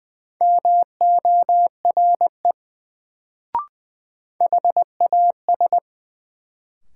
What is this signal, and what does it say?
Morse code, heard as a keyed tone, about 20 wpm, 700 Hz. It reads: MORE HAS
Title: Sets of 2 Words - New Word Drill - 20wpm